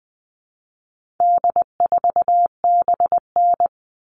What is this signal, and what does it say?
Morse code, heard as a keyed tone, about 20 wpm, 700 Hz. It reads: D4BN
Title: All Cut Numbers - 4-Digit 20wpm